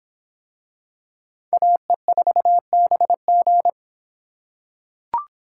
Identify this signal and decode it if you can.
Morse code, heard as a keyed tone, about 24 wpm, 700 Hz. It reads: AE4BG